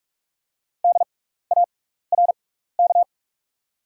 Morse code, heard as a keyed tone, about 48 wpm, spaced as for 16 wpm, 690 Hz. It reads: DARK
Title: 6x Character Spacing - Top 500 Words - 45wpm